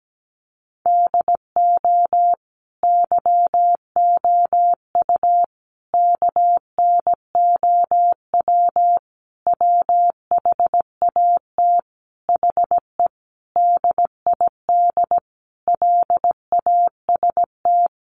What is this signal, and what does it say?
Morse code, heard as a keyed tone, about 17 wpm, 700 Hz. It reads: DO YOU KNOW WHAT HE DID LAST
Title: Sentences from Top 300 Words 17wpm